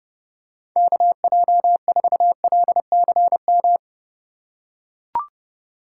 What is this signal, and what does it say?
Morse code, heard as a keyed tone, about 30 wpm, 700 Hz. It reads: KJ4LCM
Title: USA Call Signs - 6x Character Spacing - 30wpm